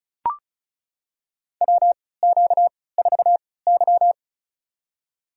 Morse code, heard as a keyed tone, about 36 wpm, spaced as for 20 wpm, 700 Hz. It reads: WQ4Y
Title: USA Call Signs - 4 Characters - 3x Character Spacing - 35wpm